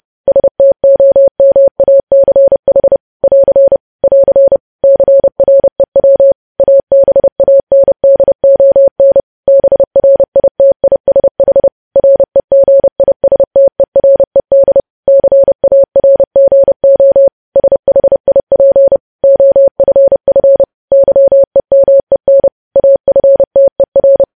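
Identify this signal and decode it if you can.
Morse code, heard as a keyed tone, about 30 wpm, 550 Hz. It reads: STOMACH <AR> <AR> CREW ABANDON BRITISH REGISTERED CARGO SHIP OFF YEMEN AFTER
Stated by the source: Morse code news headlines sent at 30 wpm.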